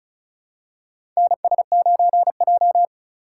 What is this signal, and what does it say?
Morse code, heard as a keyed tone, about 34 wpm, 700 Hz. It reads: NS9J
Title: USA Call Signs - 2x Character Spacing - 35wpm